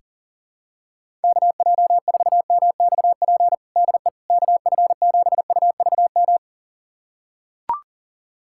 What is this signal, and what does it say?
Morse code, heard as a keyed tone, about 40 wpm, 700 Hz. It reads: KJ4MXP DE KF7UVM
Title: USA Callsign DE Callsign 40wpm